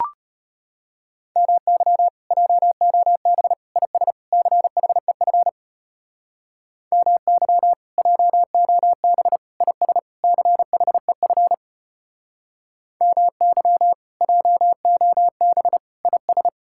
Morse code, heard as a keyed tone, about 32 wpm, 700 Hz. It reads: MY JOB IS CHEF MY JOB IS CHEF MY JOB IS